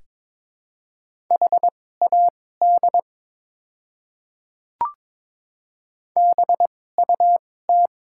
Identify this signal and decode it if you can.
Morse code, heard as a keyed tone, about 22 wpm, spaced as for 16 wpm, 700 Hz. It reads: HAD BUT
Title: 2x Character Spacing - Top 100 Words - 22wpm